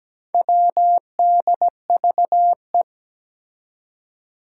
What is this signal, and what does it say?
Morse code, heard as a keyed tone, about 17 wpm, 700 Hz. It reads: WDVE
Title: Four Challenging Letter-Number Combinations 17wpm